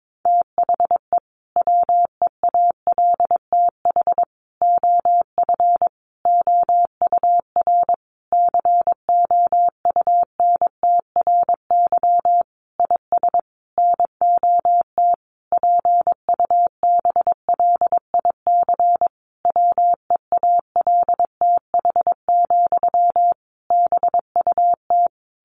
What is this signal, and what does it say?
Morse code, heard as a keyed tone, about 22 wpm, 700 Hz. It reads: THE WEALTH OF OUR COUNTRY IS NOT PUBLIC WEALTH, BUT
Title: Sentences from Top 1200 Words 22wpm